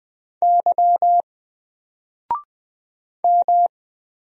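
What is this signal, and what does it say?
Morse code, heard as a keyed tone, about 20 wpm, 700 Hz. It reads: Y M
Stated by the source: Single letter and number Morse drill at 20 wpm, from N4LCD.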